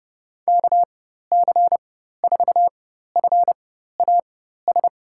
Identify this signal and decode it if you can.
Morse code, heard as a keyed tone, about 32 wpm, spaced as for 14 wpm, 700 Hz. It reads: KC4FAS
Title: USA Call Signs - 4x Character Spacing - 30wpm